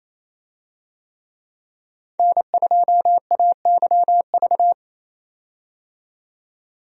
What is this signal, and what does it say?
Morse code, heard as a keyed tone, about 27 wpm, 700 Hz. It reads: N2AYV
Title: USA Call Signs - 5 Characters - 28wpm